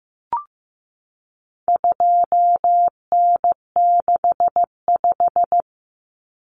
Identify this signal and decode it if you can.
Morse code, heard as a keyed tone, about 15 wpm, 700 Hz. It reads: 2N65
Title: Common Cut Numbers - 4-Digit 15wpm